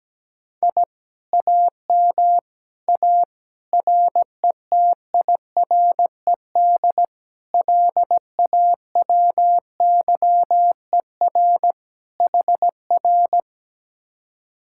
Morse code, heard as a keyed tone, about 17 wpm, 700 Hz. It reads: I AM A RETIRED LAWYER HR
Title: Speed-Racing Common QSO Phrases - 17wpm